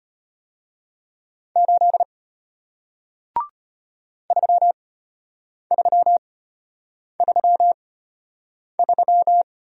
Morse code, heard as a keyed tone, about 34 wpm, spaced as for 8 wpm, 700 Hz. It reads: 8 3333